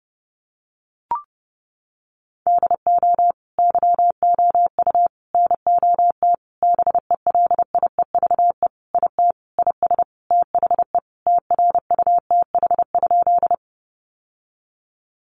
Morse code, heard as a keyed tone, about 30 wpm, 700 Hz. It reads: DO YOU NOT BELIEVE IT IS THE TRUTH?
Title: Sentences from Top 1000 Words 30wpm